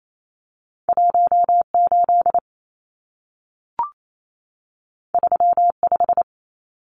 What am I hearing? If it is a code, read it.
Morse code, heard as a keyed tone, about 28 wpm, 700 Hz. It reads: 18 35
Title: Numbers - 2-Digit 28wpm